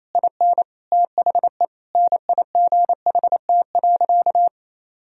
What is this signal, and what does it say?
Morse code, heard as a keyed tone, about 28 wpm, 700 Hz. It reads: IN THE NIGHT.
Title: Sentences from Top 1000 Words - Review Part 4 - 28wpm